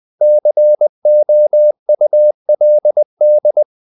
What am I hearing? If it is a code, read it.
Morse code, heard as a keyed tone, about 20 wpm, 600 Hz. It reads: COULD
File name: the old man and the sea / the old man and the sea part 1 20wpm